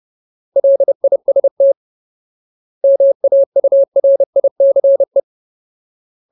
Morse code, heard as a keyed tone, about 30 wpm, 550 Hz. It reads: LIST MAURICE
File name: Ultimate Word List / 30WPM Ultimate